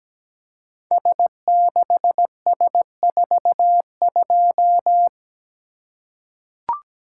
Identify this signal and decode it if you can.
Morse code, heard as a keyed tone, about 17 wpm, 700 Hz. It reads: S6S42
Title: Code Groups - 5 Letter:Numbers - 17wpm